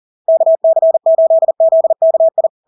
Morse code, heard as a keyed tone, about 40 wpm, 650 Hz. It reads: KC8ZKI
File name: LARK-FD-2022-Log-in-CW-40wpm-650-Hz